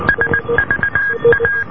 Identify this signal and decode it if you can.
Morse code, heard as a keyed tone, about 19 wpm, 1600 Hz, with noise. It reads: SVA